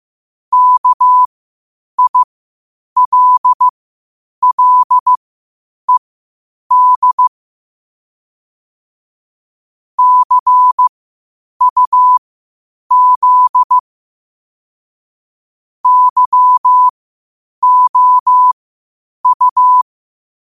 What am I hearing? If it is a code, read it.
Morse code, heard as a keyed tone, about 15 wpm, spaced as for 8 wpm, 1000 Hz. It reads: KILLED CUZ YOU
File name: qotdfiles / 8wpm